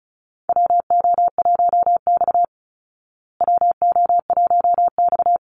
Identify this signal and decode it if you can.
Morse code, heard as a keyed tone, about 34 wpm, 700 Hz. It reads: WO1X WO1X
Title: Call Signs Repeated 2x - 35wpm